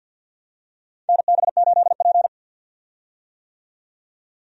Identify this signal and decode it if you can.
Morse code, heard as a keyed tone, about 52 wpm, 700 Hz. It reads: NB8P
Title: USA Call Signs - 4 Characters - 50wpm